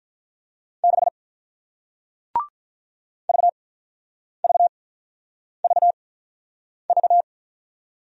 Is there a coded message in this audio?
Yes: Morse code from 0.8 to 7.2 s, heard as a keyed tone, about 48 wpm, spaced as for 9 wpm, 700 Hz. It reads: 6 VVVV